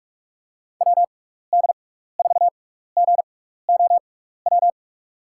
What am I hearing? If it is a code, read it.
Morse code, heard as a keyed tone, about 44 wpm, spaced as for 15 wpm, 700 Hz. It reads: WD4GOW
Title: USA Call Signs - 6 Characters - 6x Character Spacing - 45wpm